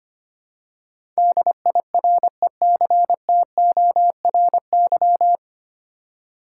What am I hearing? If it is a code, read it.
Morse code, heard as a keyed tone, about 25 wpm, 700 Hz. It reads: DIRECTORY
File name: practice / 25wpm / 9 Letter Words - First 3 Letters Given - 25wpm